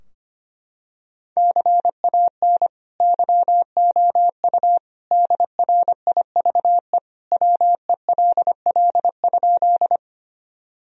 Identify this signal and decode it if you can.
Morse code, heard as a keyed tone, about 25 wpm, 700 Hz. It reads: CAN YOU DRIVE WELL?
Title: Sentences from Top 1000 Words - Review Part 2 - 25wpm